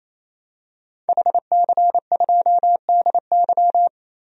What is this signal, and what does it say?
Morse code, heard as a keyed tone, about 27 wpm, 700 Hz. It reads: HC2DY
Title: DXCC-100 Call Signs - 5 Characters - Repeated 3x - 28wpm